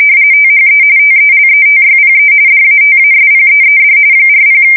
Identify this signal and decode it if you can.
RTTY, about 45.45 baud. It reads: THE TELETYPE DATASPEED 40
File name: fsk170 / teletype-corp